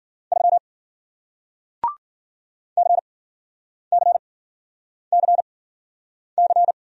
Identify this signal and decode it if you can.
Morse code, heard as a keyed tone, about 52 wpm, spaced as for 9 wpm, 700 Hz. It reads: 3 CCCC